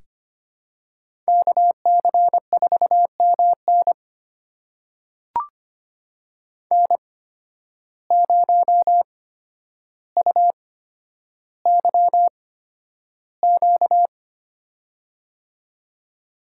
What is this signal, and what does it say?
Morse code, heard as a keyed tone, about 25 wpm, spaced as for 11 wpm, 700 Hz. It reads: KC4MN N0UYQ
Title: USA Call Signs - 5 Characters - 8x Character Spacing - 25wpm